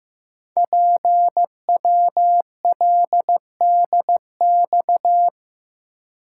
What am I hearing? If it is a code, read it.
Morse code, heard as a keyed tone, about 15 wpm, 700 Hz. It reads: PWLDX
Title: Code Groups - 5 Letter:Numbers - 15wpm